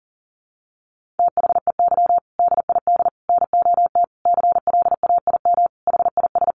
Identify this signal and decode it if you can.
Morse code, heard as a keyed tone, about 40 wpm, 700 Hz. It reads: THEY DID NOT CLAIM HIS
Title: Sentences from Top 1500 Words 40wpm